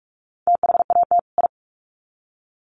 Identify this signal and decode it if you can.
Morse code, heard as a keyed tone, about 44 wpm, 700 Hz. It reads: THAT I
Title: Sets of 2 Words 45wpm